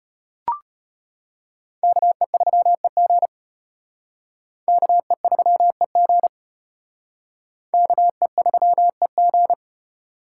Morse code, heard as a keyed tone, about 32 wpm, 700 Hz. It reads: KE3EG KE3EG KE3EG